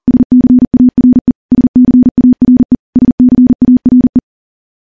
Morse code, heard as a keyed tone, about 42 wpm, 250 Hz. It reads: SCARE SCARE SCARE
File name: morse / morse-training-40WPM